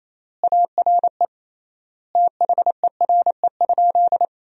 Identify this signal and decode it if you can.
Morse code, heard as a keyed tone, about 28 wpm, 700 Hz. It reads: ARE THERE?
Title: 3x Word-Spacing - Sentences from Top 100 Words - 28wpm